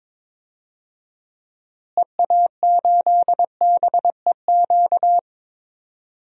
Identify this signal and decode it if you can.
Morse code, heard as a keyed tone, about 21 wpm, 700 Hz. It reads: EA8BEQ
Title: DXCC-100 Call Signs - 6 Characters - Repeated 2x - 22wpm